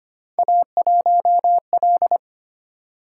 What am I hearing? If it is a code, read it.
Morse code, heard as a keyed tone, about 25 wpm, 700 Hz. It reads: A1L
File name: Three Letter-Number Combinations - 25wpm